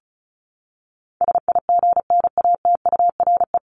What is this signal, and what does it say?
Morse code, heard as a keyed tone, about 34 wpm, 700 Hz. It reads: SIGNATURE
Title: Top 1751-2000 Words 35wpm